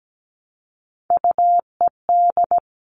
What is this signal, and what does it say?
Morse code, heard as a keyed tone, about 17 wpm, 700 Hz. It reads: UED
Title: All Cut Numbers - 3-Digit 17wpm